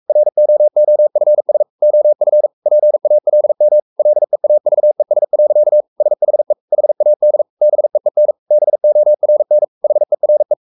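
Morse code, heard as a keyed tone, about 44 wpm, 600 Hz. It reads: ROOFS OF PALM LEAVES. SHE HAD BEEN BORN HERE
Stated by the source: Morse code at 43 wpm sending Ken Follett's novel Never in English, from CW med Gnister.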